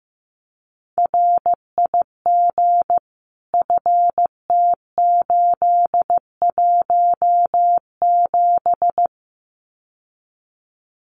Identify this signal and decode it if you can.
Morse code, heard as a keyed tone, about 15 wpm, 700 Hz. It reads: RIG FT817
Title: Common QSO Phrases Encore 15wpm